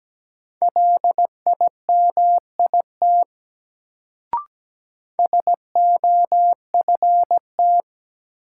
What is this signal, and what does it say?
Morse code, heard as a keyed tone, about 17 wpm, 700 Hz. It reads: LIMIT SOFT
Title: Sentences from Top 1500 Words - New Word Drill - 17wpm